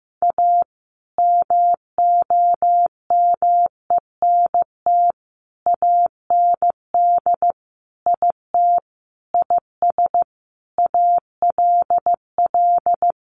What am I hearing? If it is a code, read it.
Morse code, heard as a keyed tone, about 15 wpm, 700 Hz. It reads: A MOMENT AND IT IS ALL
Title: Sentences from Top 1000 Words 15wpm